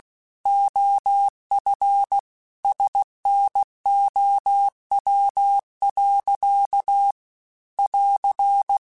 Morse code, heard as a keyed tone, about 16 wpm, 800 Hz. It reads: OF SNOW. <AR>